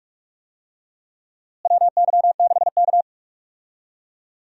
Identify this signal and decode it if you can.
Morse code, heard as a keyed tone, about 46 wpm, 700 Hz. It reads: WY6K
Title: USA Call Signs - 4 Characters - Repeated 3x - 45wpm